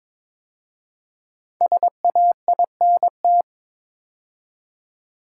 Morse code, heard as a keyed tone, about 22 wpm, 700 Hz. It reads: SAINT
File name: 5 Letter Words - First 3 Letters Given - 22wpm